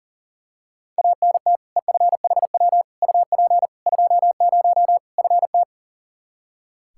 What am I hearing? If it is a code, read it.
Morse code, heard as a keyed tone, about 40 wpm, 700 Hz. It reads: ANT EFHW UP 20 FT
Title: Practice QSO - Exchange 2 - Info - 40wpm